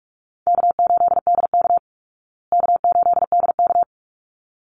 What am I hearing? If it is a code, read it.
Morse code, heard as a keyed tone, about 44 wpm, 700 Hz. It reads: K8DK K8DK
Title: Easy Call Signs Repeated 2x - 45wpm